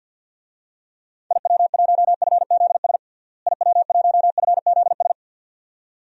Morse code, heard as a keyed tone, about 52 wpm, 700 Hz. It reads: IW1FZS IW1FZS
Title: DXCC-100 Call Signs - 6 Characters - Repeated 2x - 50wpm